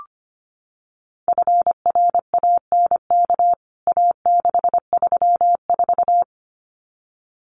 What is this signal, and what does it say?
Morse code, heard as a keyed tone, about 25 wpm, 700 Hz. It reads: FRANK A634
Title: CWT Exchange 25wpm